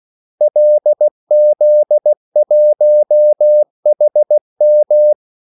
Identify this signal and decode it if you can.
Morse code, heard as a keyed tone, about 16 wpm, 600 Hz. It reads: LZ1HM
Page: Kaldesignaler 16 wpm